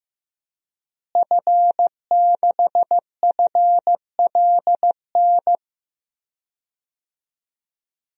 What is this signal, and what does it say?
Morse code, heard as a keyed tone, about 15 wpm, 700 Hz. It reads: F6FLN
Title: DXCC-100 Call Signs - 5 Characters - Repeated 3x - 15wpm